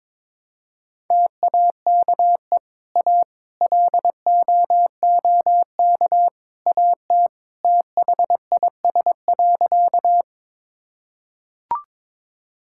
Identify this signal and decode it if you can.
Morse code, heard as a keyed tone, about 22 wpm, 700 Hz. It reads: TAKE A LOOK AT THIS.
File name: practice / 22wpm / Sentences with 3-Word Phrases - 22wpm